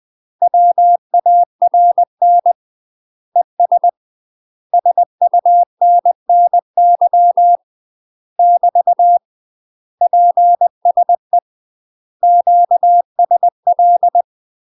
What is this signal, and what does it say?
Morse code, heard as a keyed tone, about 20 wpm, 700 Hz. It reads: WARN ES SUNNY <BT> PSE QSL
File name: UK MASTER QSOs TRAINING / 20WPM / UK MASTER QSOs 0040